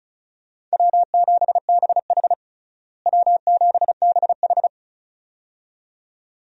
Morse code, heard as a keyed tone, about 36 wpm, 700 Hz. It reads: W7BH W7BH
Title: USA Call Signs - 4 Characters - Repeated 2x - 35wpm